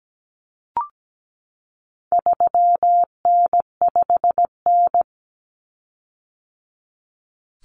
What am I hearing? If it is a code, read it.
Morse code, heard as a keyed tone, about 17 wpm, 700 Hz. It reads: 3N5N
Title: Common Cut Numbers - 4-Digit 17wpm